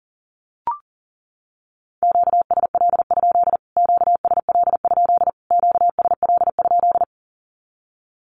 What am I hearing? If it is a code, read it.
Morse code, heard as a keyed tone, about 40 wpm, 700 Hz. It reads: QSL? QSL? QSL?
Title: Q-Codes Repeated 3x 40wpm